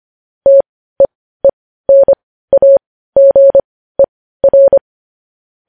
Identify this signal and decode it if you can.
Morse code, heard as a keyed tone, about 25 wpm, spaced as for 15 wpm, 550 Hz. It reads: TEENAGER
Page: news-headlines-in-morse-code-at-15-wpm